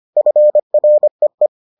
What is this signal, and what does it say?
Morse code, heard as a keyed tone, about 25 wpm, 600 Hz. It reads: FREE